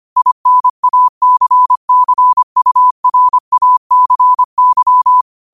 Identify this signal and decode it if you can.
Morse code, heard as a keyed tone, about 25 wpm, 1000 Hz. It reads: INACCURACY
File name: qotdfiles / 25wpm